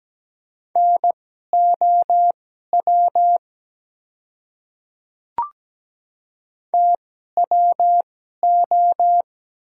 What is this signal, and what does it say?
Morse code, heard as a keyed tone, about 17 wpm, spaced as for 12 wpm, 700 Hz. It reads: NOW TWO
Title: 2x Character Spacing - Top 100 Words - 17wpm